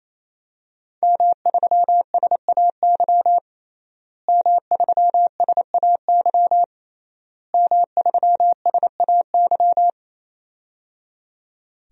Morse code, heard as a keyed tone, about 28 wpm, 700 Hz. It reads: M3SAY M3SAY M3SAY
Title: DXCC-100 Call Signs - 5 Characters - Repeated 3x - 28wpm